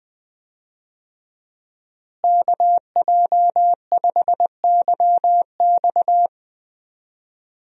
Morse code, heard as a keyed tone, about 20 wpm, 700 Hz. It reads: KJ5YX
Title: USA Call Signs - 5 Characters - Repeated 2x - 20wpm